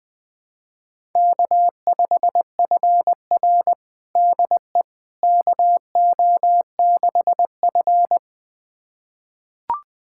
Morse code, heard as a keyed tone, about 20 wpm, 700 Hz. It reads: K5FR DE KO6F
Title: USA Callsign DE Callsign - 4 Characters - 20wpm